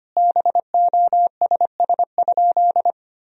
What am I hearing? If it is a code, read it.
Morse code, heard as a keyed tone, about 25 wpm, 700 Hz. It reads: BOSS?
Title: Sentences from Top 1000 Words - Review Part 4 - 25wpm